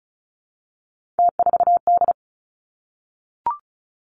Morse code, heard as a keyed tone, about 36 wpm, 700 Hz. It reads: T4D
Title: Three Challenging Letter-Number Combinations 35wpm